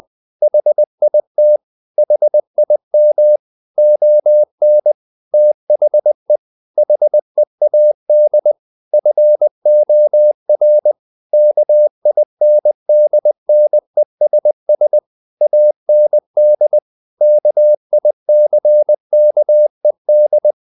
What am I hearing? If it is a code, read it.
Morse code, heard as a keyed tone, about 20 wpm, 600 Hz. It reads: HIT HIM ON THE HEAD FOR KINDNESS AND KICKED
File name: the old man and the sea / the old man and the sea part 1 20wpm